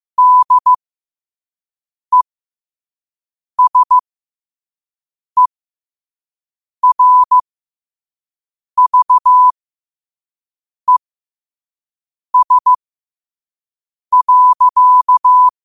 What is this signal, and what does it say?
Morse code, heard as a keyed tone, about 15 wpm, spaced as for 5 wpm, 1000 Hz. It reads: DESERVES.
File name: qotdfiles / 5wpm